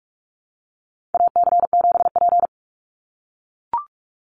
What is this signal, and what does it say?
Morse code, heard as a keyed tone, about 44 wpm, 700 Hz. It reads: AC7P
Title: NA SOTA Chaser Callsigns 45wpm